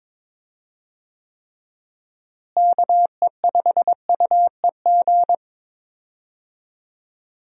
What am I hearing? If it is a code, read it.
Morse code, heard as a keyed tone, about 22 wpm, 700 Hz. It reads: KE5UEG
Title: USA Call Signs - 6 Characters - Repeated 2x - 22wpm